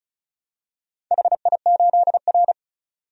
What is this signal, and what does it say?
Morse code, heard as a keyed tone, about 36 wpm, 700 Hz. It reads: HI8R
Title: DXCC-100 Call Signs - 4 Characters - 35wpm